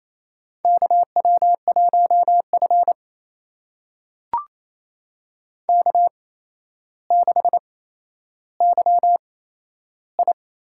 Morse code, heard as a keyed tone, about 28 wpm, spaced as for 12 wpm, 700 Hz. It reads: KW1F K6YI
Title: USA Call Signs - 4 Characters - 8x Character Spacing - 28wpm